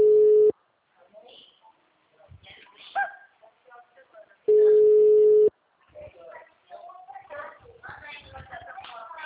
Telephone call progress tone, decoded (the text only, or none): ringback tone